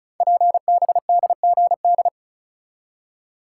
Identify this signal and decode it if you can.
Morse code, heard as a keyed tone, about 36 wpm, 700 Hz. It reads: PBDGD
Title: Code Groups - 5 Letters - 35wpm